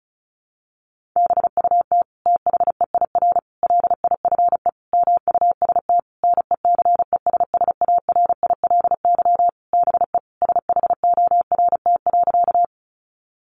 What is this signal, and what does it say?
Morse code, heard as a keyed tone, about 34 wpm, 700 Hz. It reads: BUT THEIR LIFE MUST NECESSARILY BE SHORT.
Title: Sentences from Top 1400 Words 35wpm